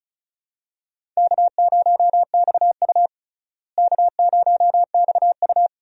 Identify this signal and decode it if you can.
Morse code, heard as a keyed tone, about 34 wpm, 700 Hz. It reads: K0XU K0XU
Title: USA Call Signs - 4 Characters - Repeated 2x - 35wpm